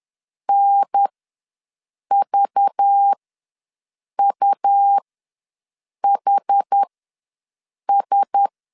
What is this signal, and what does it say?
Morse code, heard as a keyed tone, about 10 wpm, spaced as for 6 wpm, 800 Hz. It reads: NVUHS
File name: LEZIONI / 5 LBVC / 5 LBV ws 10wpm 3645